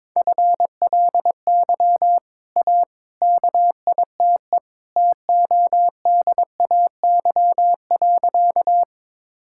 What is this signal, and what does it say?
Morse code, heard as a keyed tone, about 21 wpm, 700 Hz. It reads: FLY A KITE TODAY.
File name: practice / 22wpm / Easy Sentences with Hints - 22wpm